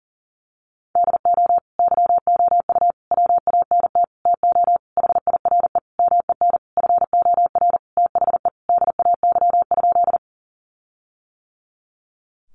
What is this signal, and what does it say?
Morse code, heard as a keyed tone, about 40 wpm, 700 Hz. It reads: DO YOU WANT TO HIRE MEN FOR THE DAY?
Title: Sentences from Top 1400 Words 40wpm